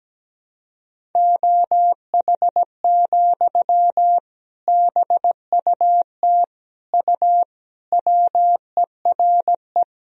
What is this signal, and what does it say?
Morse code, heard as a keyed tone, about 17 wpm, 700 Hz. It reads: OH, BUT U WERE
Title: Sentences from Top 500 Words and Abbreviations 17wpm